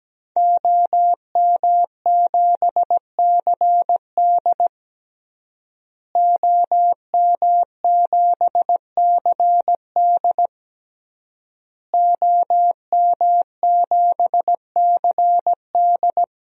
Morse code, heard as a keyed tone, about 17 wpm, 700 Hz. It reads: OM7CD OM7CD OM7CD
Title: DXCC-100 Call Signs - 5 Characters - Repeated 3x - 17wpm